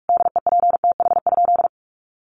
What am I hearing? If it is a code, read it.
Morse code, heard as a keyed tone, about 44 wpm, 700 Hz. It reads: DEPTH?
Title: Sentences from Top 1200 Words 45wpm